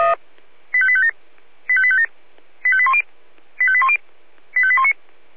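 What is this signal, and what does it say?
Five-tone selective call (ZVEI), heard as quick runs of five tones, tones of about 70 ms. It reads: 85857 85857 85810 85810 85819
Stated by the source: ZVEI five-tone selective call received on an SDR.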